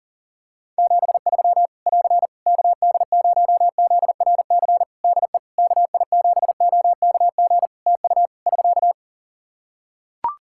Morse code, heard as a keyed tone, about 40 wpm, 700 Hz. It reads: 73 <AR> KD0ZRC DE KI7OKG TU <SK>
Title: K1USN SST QSO Practice 40wpm